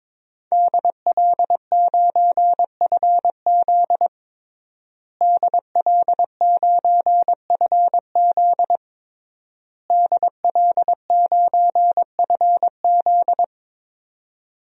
Morse code, heard as a keyed tone, about 23 wpm, 700 Hz. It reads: DL9FZ DL9FZ DL9FZ
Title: DXCC-100 Call Signs - 5 Characters - Repeated 3x - 22wpm